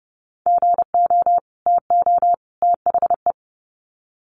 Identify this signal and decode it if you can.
Morse code, heard as a keyed tone, about 30 wpm, 700 Hz. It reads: GO TO THE
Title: Sets of 3 Words 30wpm